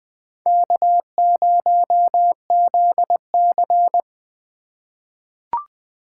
Morse code, heard as a keyed tone, about 20 wpm, 700 Hz. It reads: K0ZC
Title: USA Call Signs - 4 Characters - 2x Character Spacing - 20wpm